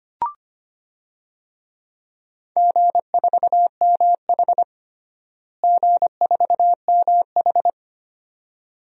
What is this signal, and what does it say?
Morse code, heard as a keyed tone, about 25 wpm, 700 Hz. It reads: G4MH G4MH
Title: DXCC-100 Call Signs - 4 Characters - Repeated 2x - 25wpm